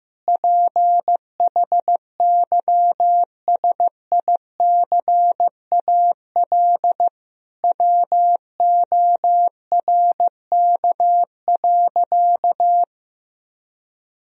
Morse code, heard as a keyed tone, about 15 wpm, 700 Hz. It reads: PHYSICAL WORK.
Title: Sentences from Top 1000 Words - Review Part 4 - 15wpm